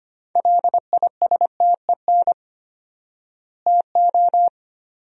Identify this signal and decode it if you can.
Morse code, heard as a keyed tone, about 25 wpm, 700 Hz. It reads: LISTEN TO
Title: 4x Word-Spacing - Sentences from Top 400 Words - 25wpm